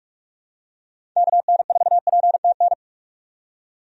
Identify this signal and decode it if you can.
Morse code, heard as a keyed tone, about 44 wpm, 700 Hz. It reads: KN4PTN
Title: USA Call Signs - 6 Characters - 45wpm